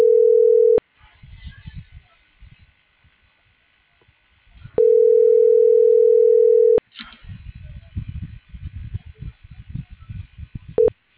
Ringback tone (North America), 440 + 480 Hz.